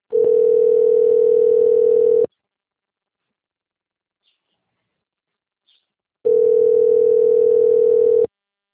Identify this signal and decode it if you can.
Ringback tone (North America), 440 + 480 Hz.